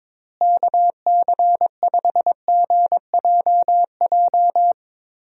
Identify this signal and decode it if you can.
Morse code, heard as a keyed tone, about 22 wpm, 700 Hz. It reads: KC5GJJ
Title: USA Call Signs - 6 Characters - Repeated 2x - 22wpm